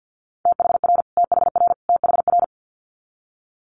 Morse code, heard as a keyed tone, about 50 wpm, 690 Hz. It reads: THR THR THR
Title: Multi-letter Phonemes Repeated 3x 50wpm